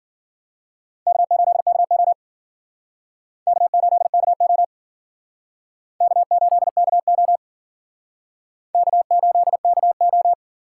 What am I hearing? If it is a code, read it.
Morse code, heard as a keyed tone, about 51 wpm, 700 Hz. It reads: K8KO K8KO K8KO K8KO